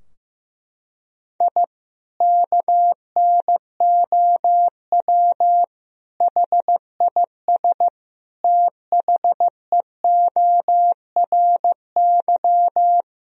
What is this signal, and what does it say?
Morse code, heard as a keyed tone, about 15 wpm, 700 Hz. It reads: I KNOW HIS THEORY
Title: Sentences from Top 1000 Words - Review Part 3 - 15wpm